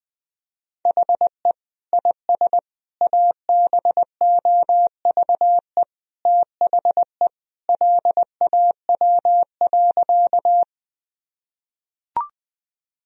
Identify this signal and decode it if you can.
Morse code, heard as a keyed tone, about 20 wpm, 700 Hz. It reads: HE IS ABOVE THE LAW.
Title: Sentences from Top 1000 Words - Review Part 3 - 20wpm